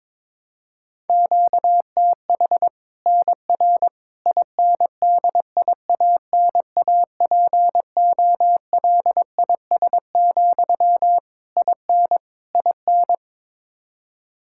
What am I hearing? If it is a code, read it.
Morse code, heard as a keyed tone, about 22 wpm, 700 Hz. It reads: QTH NR INDIANAPOLIS, IN IN
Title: Practice QSO - Exchange 1 - Beginning - 22wpm